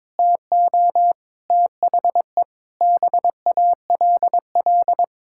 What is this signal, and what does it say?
Morse code, heard as a keyed tone, about 22 wpm, 700 Hz. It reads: TO THE BALL
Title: Sentences from Top 1000 Words - Review Part 6 - 22wpm